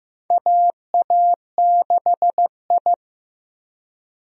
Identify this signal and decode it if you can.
Morse code, heard as a keyed tone, about 15 wpm, 700 Hz. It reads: AA6I
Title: USA Call Signs - 4 Characters - Repeated 3x - 15wpm